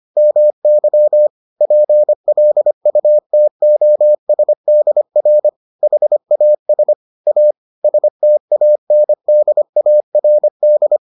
Morse code, heard as a keyed tone, about 25 wpm, 600 Hz. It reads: MY PLUTOSDR HAS A STANDARD